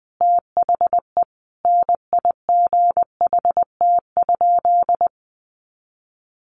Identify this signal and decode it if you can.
Morse code, heard as a keyed tone, about 20 wpm, 700 Hz. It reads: THE NIGHT?
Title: Sentences from Top 1300 Words 20wpm